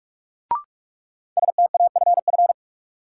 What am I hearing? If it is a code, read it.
Morse code, heard as a keyed tone, about 46 wpm, 700 Hz. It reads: STAFF
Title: Sentences from Top 1200 Words - New Word Drill - 45wpm